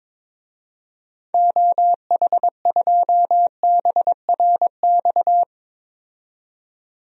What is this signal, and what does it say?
Morse code, heard as a keyed tone, about 22 wpm, 700 Hz. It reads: OH2BRX
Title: DXCC-100 Call Signs - 6 Characters - 22wpm